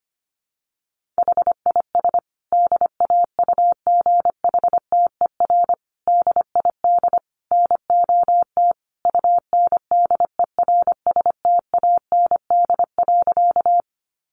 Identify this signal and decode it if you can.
Morse code, heard as a keyed tone, about 25 wpm, 700 Hz. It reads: HIS DAUGHTER DID NOT UNDERSTAND.
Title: Sentences from Top 1500 Words 25wpm